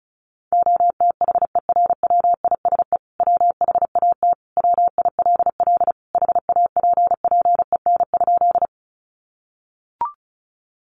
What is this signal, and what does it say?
Morse code, heard as a keyed tone, about 34 wpm, 700 Hz. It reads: OTHERWISE WHAT WILL HAPPEN?
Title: Sentences from Top 1200 Words 35wpm